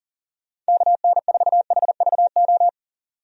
Morse code, heard as a keyed tone, about 40 wpm, 700 Hz. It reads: KN4HVO
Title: USA Call Signs - 6 Characters - 40wpm